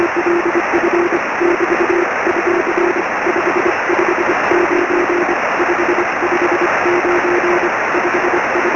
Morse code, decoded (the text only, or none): none